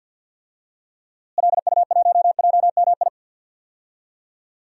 Morse code, heard as a keyed tone, about 53 wpm, 700 Hz. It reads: LU1JMI
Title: DXCC-100 Call Signs - 6 Characters - Repeated 2x - 50wpm